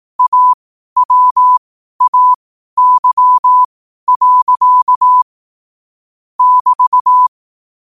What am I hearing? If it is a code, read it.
Morse code, heard as a keyed tone, about 18 wpm, spaced as for 12 wpm, 1000 Hz. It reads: AWAY. <BT>